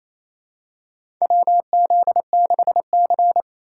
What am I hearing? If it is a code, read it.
Morse code, heard as a keyed tone, about 27 wpm, 700 Hz. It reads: WZ6C
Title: USA Call Signs - 3x Character Spacing - 28wpm